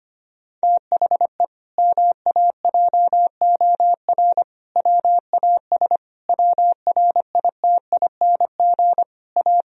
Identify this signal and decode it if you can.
Morse code, heard as a keyed tone, about 25 wpm, 700 Hz. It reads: THE MAJOR WAS WRITING A
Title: Sentences from Top 1000 Words - Review Part 4 - 25wpm